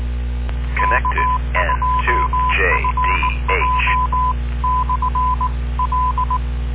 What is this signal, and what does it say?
Morse code, heard as a keyed tone, about 19 wpm, 1010 Hz, with noise. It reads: K2FJ/L